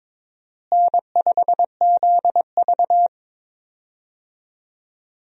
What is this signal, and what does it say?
Morse code, heard as a keyed tone, about 22 wpm, 700 Hz. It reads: N5ZV
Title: USA Call Signs - 4 Characters - Repeated 3x - 22wpm